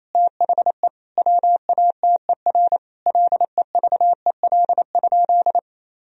Morse code, heard as a keyed tone, about 28 wpm, 700 Hz. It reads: THE WATER LEVEL?
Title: Sentences from Top 1000 Words - Review Part 4 - 28wpm